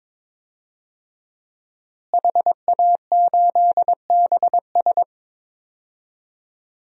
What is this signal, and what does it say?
Morse code, heard as a keyed tone, about 22 wpm, 700 Hz. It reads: HA8BS
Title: DXCC-100 Call Signs - 5 Characters - Repeated 2x - 22wpm